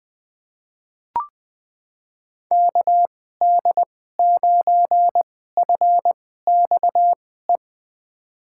Morse code, heard as a keyed tone, about 20 wpm, spaced as for 14 wpm, 700 Hz. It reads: KD9FXE
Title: USA Call Signs - 6 Characters - 2x Character Spacing - 20wpm